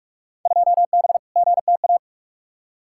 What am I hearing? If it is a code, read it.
Morse code, heard as a keyed tone, about 46 wpm, 700 Hz. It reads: 2B GTA
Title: Field Day Exchanges Encore - 45wpm